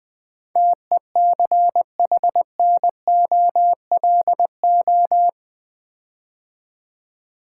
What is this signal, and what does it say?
Morse code, heard as a keyed tone, about 20 wpm, 700 Hz. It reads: TECHNOLO
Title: Build A Word - 4 Letters At A Time - 10-Letter Words - 20wpm